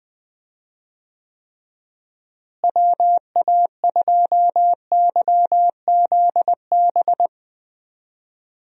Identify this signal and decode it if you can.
Morse code, heard as a keyed tone, about 20 wpm, 700 Hz. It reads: WA2YZB
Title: USA Call Signs - 6 Characters - Repeated 3x - 20wpm